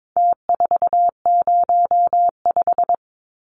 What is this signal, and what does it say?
Morse code, heard as a keyed tone, about 22 wpm, 700 Hz. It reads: T405
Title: Numbers - Contest Style - 4-Digit 22wpm